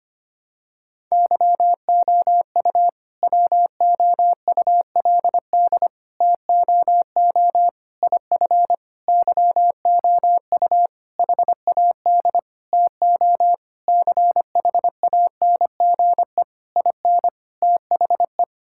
Morse code, heard as a keyed tone, about 25 wpm, 700 Hz. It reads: YOU WOULD TOO IF YOU HAD TO CHANGE IN THE
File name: practice / 25wpm / All-Age Appropriate Jokes - 25wpm